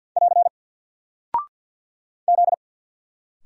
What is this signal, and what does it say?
Morse code, heard as a keyed tone, about 50 wpm, 700 Hz. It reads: <AR> Z
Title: Single Character (Letter:Number:Punctuation:Prosign) 50wpm